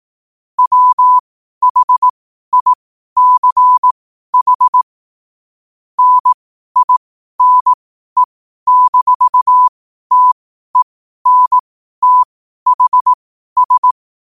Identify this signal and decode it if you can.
Morse code, heard as a keyed tone, about 18 wpm, spaced as for 12 wpm, 1000 Hz. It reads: WHICH NINE-TENTHS
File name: qotdfiles / 12wpm